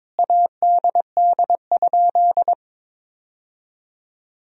Morse code, heard as a keyed tone, about 21 wpm, 700 Hz. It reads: ADD?
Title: Sentences from Top 1000 Words - Review Part 1 - 22wpm